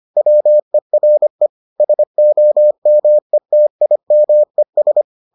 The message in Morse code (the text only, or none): WERE SOMETIMES